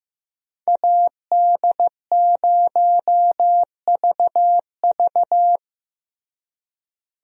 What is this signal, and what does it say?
Morse code, heard as a keyed tone, about 15 wpm, 700 Hz. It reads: AD0VV
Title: USA Call Signs - 5 Characters - Repeated 2x - 15wpm